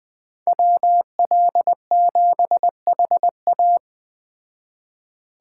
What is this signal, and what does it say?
Morse code, heard as a keyed tone, about 20 wpm, 700 Hz. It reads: WL7HA
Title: Speed Racing - USA 2x2 Call Signs 17wpm